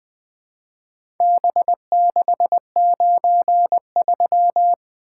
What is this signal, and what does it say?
Morse code, heard as a keyed tone, about 21 wpm, 700 Hz. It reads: B693
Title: Four Letter-Number Combinations - 20wpm